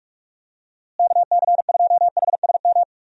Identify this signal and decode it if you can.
Morse code, heard as a keyed tone, about 42 wpm, 700 Hz. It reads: KC2HSM
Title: USA Call Signs - 6 Characters - Repeated 2x - 45wpm